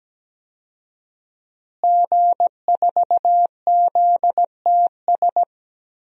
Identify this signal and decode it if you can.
Morse code, heard as a keyed tone, about 17 wpm, 700 Hz. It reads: G4ZTS
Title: DXCC-100 Call Signs - 5 Characters - Repeated 3x - 17wpm